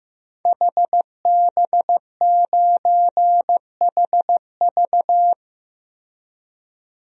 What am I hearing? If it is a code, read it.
Morse code, heard as a keyed tone, about 16 wpm, 700 Hz. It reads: HB9HV
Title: DXCC-100 Call Signs - 5 Characters - 15wpm